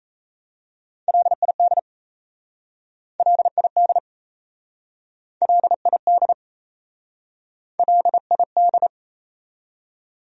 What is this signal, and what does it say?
Morse code, heard as a keyed tone, about 38 wpm, 700 Hz. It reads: LID LID LID LID